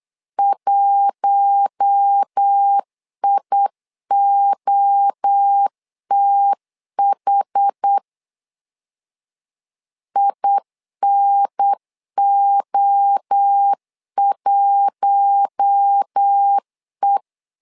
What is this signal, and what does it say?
Morse code, heard as a keyed tone, about 8 wpm, 800 Hz. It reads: 1IOTH INO1E
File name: LEZIONI / 4 Q1 8wpm 3035